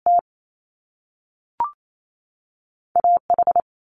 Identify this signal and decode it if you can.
Morse code, heard as a keyed tone, about 28 wpm, 700 Hz. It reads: T A H
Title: Top Two Letter Combinations in Words 28wpm